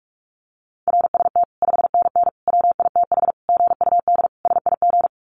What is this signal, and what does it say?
Morse code, heard as a keyed tone, about 44 wpm, 700 Hz. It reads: RST 5NN WITH GUD SIG
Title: Common QSO Phrases 45wpm